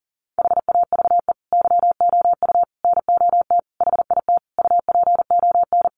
Morse code, heard as a keyed tone, about 40 wpm, 700 Hz. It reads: HAVE YOU NOT HIT UPON
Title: Sentences from Top 1200 Words 40wpm